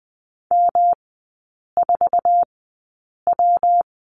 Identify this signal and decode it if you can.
Morse code, heard as a keyed tone, about 20 wpm, spaced as for 8 wpm, 700 Hz. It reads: M4W
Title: 114 Lesson 19e - Letter w - Three Characters 20wpm